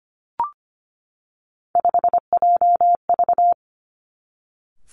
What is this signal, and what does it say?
Morse code, heard as a keyed tone, about 25 wpm, 700 Hz. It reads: 5JV
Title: Three Challenging Letter-Number Combinations 25wpm